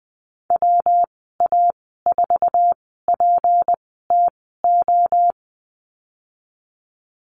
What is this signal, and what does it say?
Morse code, heard as a keyed tone, about 20 wpm, spaced as for 15 wpm, 700 Hz. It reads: WA4PTO
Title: 167 Lesson 27g - Number 7 - Call Signs 20wpm